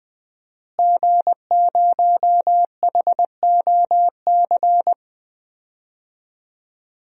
Morse code, heard as a keyed tone, about 20 wpm, 700 Hz. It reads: G0HOC
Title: DXCC-100 Call Signs - 5 Characters - Repeated 3x - 20wpm